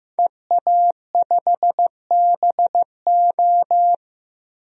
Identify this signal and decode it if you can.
Morse code, heard as a keyed tone, about 15 wpm, 700 Hz. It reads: EA5BO
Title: DXCC-100 Call Signs - 5 Characters - 15wpm